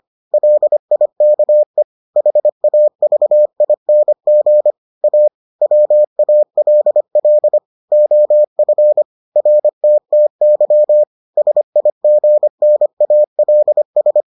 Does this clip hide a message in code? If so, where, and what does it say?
Yes: Morse code from 0.3 to 14.2 s, heard as a keyed tone, about 25 wpm, 600 Hz. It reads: LIKE HAVING A WALL OF RTTY SIGNALS